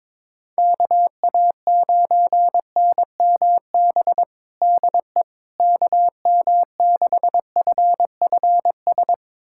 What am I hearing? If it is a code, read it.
Morse code, heard as a keyed tone, about 22 wpm, 700 Hz. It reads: KA9NMB DE KM6FFS
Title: USA Callsign DE Callsign 22wpm